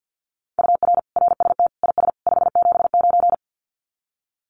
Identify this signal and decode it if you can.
Morse code, heard as a keyed tone, about 50 wpm, 700 Hz. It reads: UR RST IS 579
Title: Common QSO Phrases 50wpm